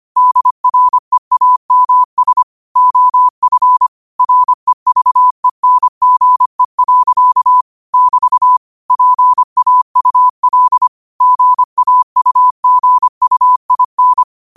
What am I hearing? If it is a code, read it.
Morse code, heard as a keyed tone, about 25 wpm, 1000 Hz. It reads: DREAMS OF REVENGE. <BT> PAUL GAUGUIN